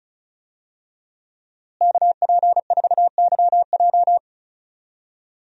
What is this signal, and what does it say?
Morse code, heard as a keyed tone, about 34 wpm, 700 Hz. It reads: KP4YJ
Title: DXCC-100 Call Signs - 5 Characters - Repeated 2x - 35wpm